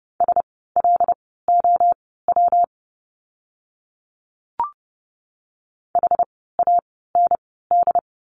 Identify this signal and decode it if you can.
Morse code, heard as a keyed tone, about 30 wpm, spaced as for 17 wpm, 700 Hz. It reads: SLOW HAND
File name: course / 30:10wpm / 115 Lesson 19f - Letter w - Words 30wpm